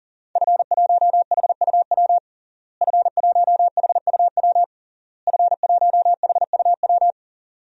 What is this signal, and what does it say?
Morse code, heard as a keyed tone, about 37 wpm, 700 Hz. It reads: F1HUW F1HUW F1HUW
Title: DXCC-100 Call Signs - 5 Characters - Repeated 3x - 40wpm